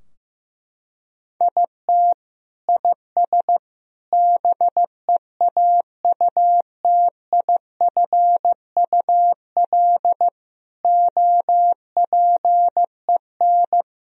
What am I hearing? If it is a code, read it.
Morse code, heard as a keyed tone, about 15 wpm, 700 Hz. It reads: IT IS BEAUTIFUL OPEN
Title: Sentences from Top 1000 Words - Review Part 2 - 15wpm